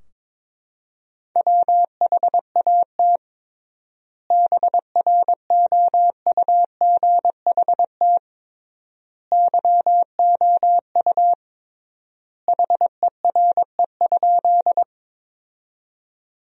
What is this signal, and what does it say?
Morse code, heard as a keyed tone, about 22 wpm, 700 Hz. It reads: WHAT BROUGHT YOU HERE?
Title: 3x Word-Spacing - Sentences from Top 500 Words - 22wpm